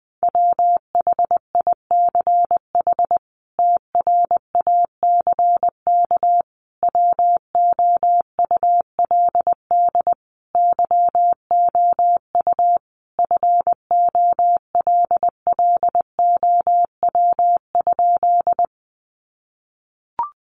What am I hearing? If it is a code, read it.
Morse code, heard as a keyed tone, about 20 wpm, 700 Hz. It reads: WHICH TRACK WOULD YOU FOLLOW?
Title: Sentences from Top 1100 Words 20wpm